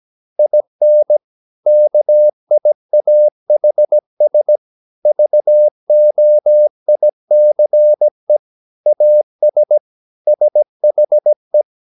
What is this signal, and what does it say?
Morse code, heard as a keyed tone, about 17 wpm, 600 Hz. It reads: IN KIAHS VOICE AS SHE
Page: Romanen Never af Ken Follet på engelsk.